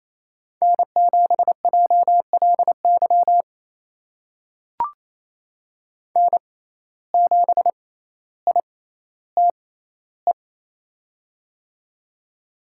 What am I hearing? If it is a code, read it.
Morse code, heard as a keyed tone, about 27 wpm, spaced as for 14 wpm, 700 Hz. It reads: N7JLY N7ITE
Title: USA Call Signs - 5 Characters - 6x Character Spacing - 28wpm